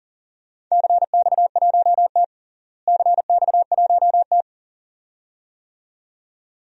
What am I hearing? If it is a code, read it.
Morse code, heard as a keyed tone, about 37 wpm, 700 Hz. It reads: CX1T CX1T
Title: DXCC-100 Call Signs - 4 Characters - Repeated 2x - 40wpm